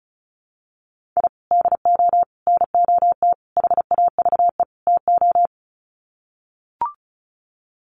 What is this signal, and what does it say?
Morse code, heard as a keyed tone, about 34 wpm, 700 Hz. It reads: I DO NOT HAVE TO
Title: Sets of 5 Words 35wpm